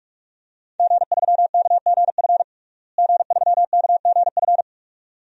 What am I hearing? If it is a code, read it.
Morse code, heard as a keyed tone, about 46 wpm, 700 Hz. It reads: G3KGF G3KGF
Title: DXCC-100 Call Signs - 5 Characters - Repeated 2x - 45wpm